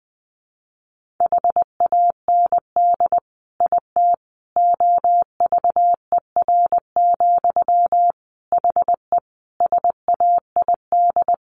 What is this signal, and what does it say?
Morse code, heard as a keyed tone, about 20 wpm, 700 Hz. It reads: HAND IT OVER, HE SAID
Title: Sentences from Top 1500 Words 20wpm